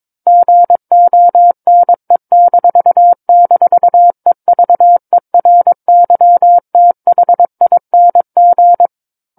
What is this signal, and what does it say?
Morse code, heard as a keyed tone, about 22 wpm, 700 Hz. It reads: GONE--EVERYTHING
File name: Book One / 13-Chapter 13 - 22 WPM